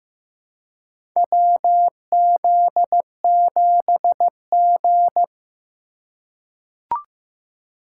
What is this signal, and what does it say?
Morse code, heard as a keyed tone, about 15 wpm, 700 Hz. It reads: WZ7G
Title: USA Call Signs - 4 Characters - 4x Character Spacing - 15wpm